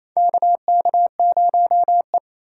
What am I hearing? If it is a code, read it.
Morse code, heard as a keyed tone, about 28 wpm, 700 Hz. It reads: KK0E
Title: USA Call Signs - 3x Character Spacing - 28wpm